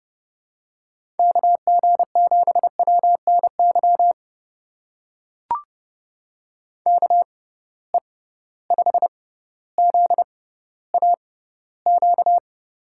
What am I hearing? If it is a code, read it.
Morse code, heard as a keyed tone, about 30 wpm, spaced as for 15 wpm, 700 Hz. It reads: KG7WNY KE5ZAQ
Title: USA Call Signs - 6 Characters - 6x Character Spacing - 30wpm